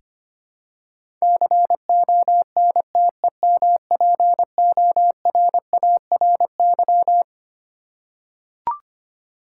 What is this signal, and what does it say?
Morse code, heard as a keyed tone, about 25 wpm, 700 Hz. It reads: CONTEMPORARY
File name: Twelve Letter Words - 25wpm